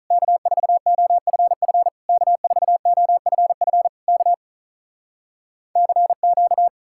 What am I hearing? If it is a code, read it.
Morse code, heard as a keyed tone, about 46 wpm, 700 Hz. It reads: K4OFF K4OFF K CQ